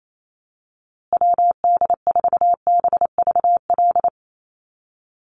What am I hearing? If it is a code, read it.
Morse code, heard as a keyed tone, about 28 wpm, 700 Hz. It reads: WD4BVL
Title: USA Call Signs Repeated 2x 28wpm